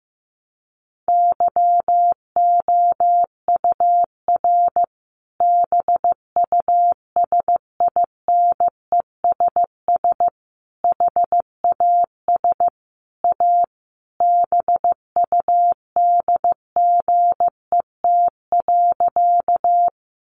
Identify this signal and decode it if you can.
Morse code, heard as a keyed tone, about 15 wpm, 700 Hz. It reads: YOUR BUSINESS HAS A BUDGET.
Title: Sentences from Top 1200 Words 15wpm